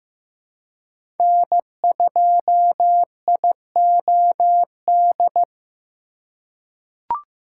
Morse code, heard as a keyed tone, about 16 wpm, 700 Hz. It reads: N2IOD
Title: USA Call Signs - 5 Characters - 4x Character Spacing - 15wpm